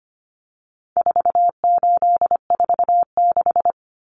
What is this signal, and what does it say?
Morse code, heard as a keyed tone, about 25 wpm, 700 Hz. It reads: 4846
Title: Common Cut Numbers - 4-Digit 25wpm